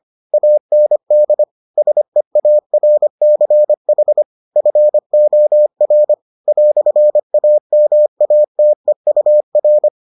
Morse code, heard as a keyed tone, about 25 wpm, 600 Hz. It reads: AND SEARCH FOR "AMATEUR